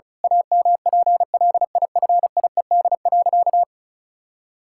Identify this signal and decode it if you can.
Morse code, heard as a keyed tone, about 36 wpm, 700 Hz. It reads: AMPLIFIED.
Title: Pangram Challenge v2 - 35wpm